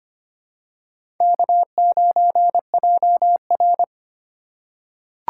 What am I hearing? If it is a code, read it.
Morse code, heard as a keyed tone, about 25 wpm, 700 Hz. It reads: K9JR
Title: USA Call Signs - 4 Characters - 2x Character Spacing - 25wpm